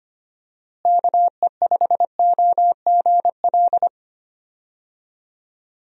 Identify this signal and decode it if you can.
Morse code, heard as a keyed tone, about 25 wpm, 700 Hz. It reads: KE5OGL
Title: USA Call Signs - 6 Characters - Repeated 2x - 25wpm